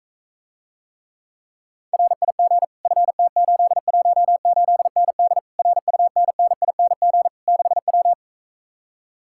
Morse code, heard as a keyed tone, about 46 wpm, 700 Hz. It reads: RIG FT818ND RUNNING 6W